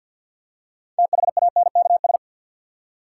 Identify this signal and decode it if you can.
Morse code, heard as a keyed tone, about 53 wpm, 700 Hz. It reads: THANKS
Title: Six Letter Words - 50wpm